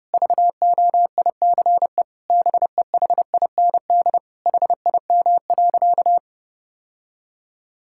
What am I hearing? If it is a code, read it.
Morse code, heard as a keyed tone, about 32 wpm, 700 Hz. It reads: VOICE BEHIND HIM.
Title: Sentences from Top 1600 Words 30wpm